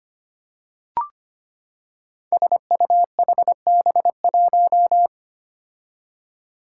Morse code, heard as a keyed tone, about 25 wpm, 700 Hz. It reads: SUHB1
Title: Five Challenging Letter-Number Combinations 25wpm